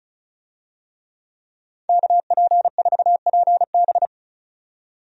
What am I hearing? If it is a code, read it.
Morse code, heard as a keyed tone, about 36 wpm, 700 Hz. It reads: KP4PB
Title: DXCC-100 Call Signs - 5 Characters - 35wpm